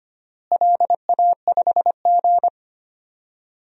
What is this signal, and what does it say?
Morse code, heard as a keyed tone, about 25 wpm, 700 Hz. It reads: LA5G
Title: DXCC-100 Call Signs - 4 Characters - Repeated 3x - 25wpm